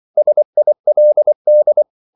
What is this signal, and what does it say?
Morse code, heard as a keyed tone, about 24 wpm, 600 Hz. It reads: SILD
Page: Korte danske ord 24wpm